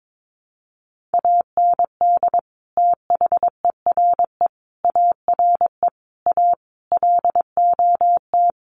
Morse code, heard as a keyed tone, about 22 wpm, 700 Hz. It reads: AND THERE ARE A LOT
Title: Sets of 5 Words Encore2 22wpm